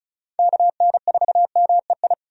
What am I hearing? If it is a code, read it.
Morse code, heard as a keyed tone, about 36 wpm, 700 Hz. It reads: KN4MEI
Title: USA Call Signs - 6 Characters - 35wpm